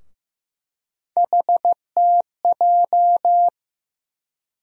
Morse code, heard as a keyed tone, about 15 wpm, 700 Hz. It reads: HTJ
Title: USA Call Sign Postfixes - 3 Letters 15wpm